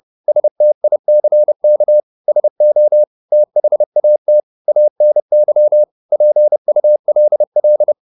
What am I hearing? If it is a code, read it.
Morse code, heard as a keyed tone, about 30 wpm, 600 Hz. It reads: STICK SO THAT ANY PULL
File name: the old man and the sea / the old man and the sea 1 30wpm